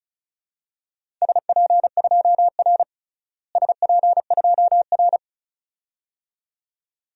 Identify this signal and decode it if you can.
Morse code, heard as a keyed tone, about 34 wpm, 700 Hz. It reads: SP2R SP2R
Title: DXCC-100 Call Signs - 4 Characters - Repeated 2x - 35wpm